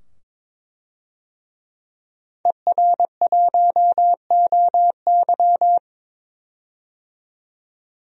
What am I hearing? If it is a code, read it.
Morse code, heard as a keyed tone, about 21 wpm, 700 Hz. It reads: ER1OY
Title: DXCC-100 Call Signs - 5 Characters - 22wpm